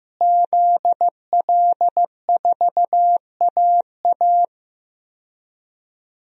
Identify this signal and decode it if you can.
Morse code, heard as a keyed tone, about 15 wpm, 700 Hz. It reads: ZL4AA
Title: DXCC-100 Call Signs - 5 Characters - Repeated 3x - 15wpm